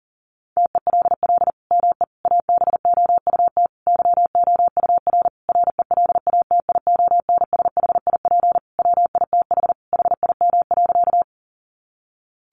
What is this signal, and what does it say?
Morse code, heard as a keyed tone, about 37 wpm, 700 Hz. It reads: TELL ME ABOUT YOUR RELATIONSHIP WITH HIM.
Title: Sentences from Top 1100 Words 40wpm